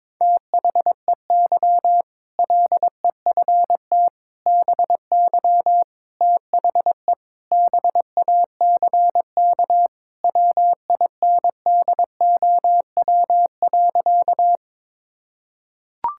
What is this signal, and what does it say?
Morse code, heard as a keyed tone, about 22 wpm, 700 Hz. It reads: THEY LEFT BY THE BACK WINDOW.
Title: Sentences from Top 1600 Words 22wpm